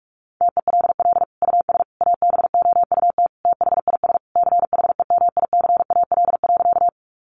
Morse code, heard as a keyed tone, about 46 wpm, 700 Hz. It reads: TELL US ABOUT THIS CHEMICAL.
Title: Sentences from Top 1200 Words 45wpm